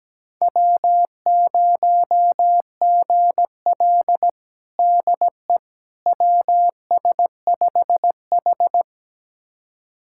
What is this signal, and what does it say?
Morse code, heard as a keyed tone, about 17 wpm, 700 Hz. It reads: W0GL DE WS5H
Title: USA Callsign DE Callsign - 4 Characters - 17wpm